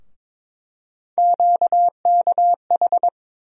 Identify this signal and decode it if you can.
Morse code, heard as a keyed tone, about 22 wpm, 700 Hz. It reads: QKH
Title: USA Call Sign Postfixes - 3 Letters 22wpm